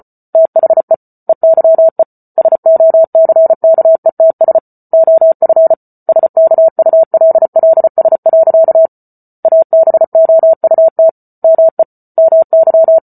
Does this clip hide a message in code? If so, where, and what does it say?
Yes: Morse code from 0.3 to 13.0 s, heard as a keyed tone, about 34 wpm, 650 Hz. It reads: THE EYE SOCKETS OF SKULLS. ABOUT ME MY